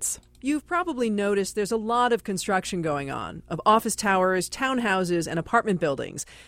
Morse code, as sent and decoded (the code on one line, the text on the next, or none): none